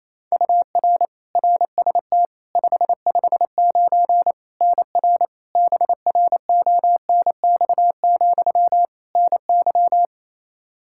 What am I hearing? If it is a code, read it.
Morse code, heard as a keyed tone, about 27 wpm, 700 Hz. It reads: UR RST 559 NR BRONX, NY
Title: Practice QSO - Exchange 1 - Beginning - 28wpm